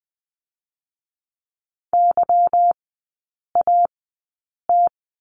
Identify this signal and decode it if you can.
Morse code, heard as a keyed tone, about 20 wpm, spaced as for 8 wpm, 690 Hz. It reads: YAT
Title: 159 Lesson 26e - Letter g - Three Characters 20wpm